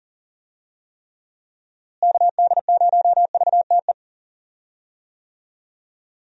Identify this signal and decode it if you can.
Morse code, heard as a keyed tone, about 40 wpm, 700 Hz. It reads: KD0VTE
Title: USA Call Signs - 6 Characters - Repeated 3x - 40wpm